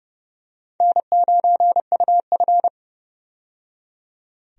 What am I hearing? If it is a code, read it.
Morse code, heard as a keyed tone, about 30 wpm, 700 Hz. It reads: N9UF
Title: USA Call Signs - 4 Characters - 30wpm